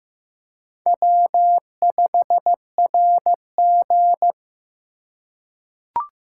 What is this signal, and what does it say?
Morse code, heard as a keyed tone, about 15 wpm, 700 Hz. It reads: W5RG
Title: USA Call Signs - 4 Characters - 3x Character Spacing - 15wpm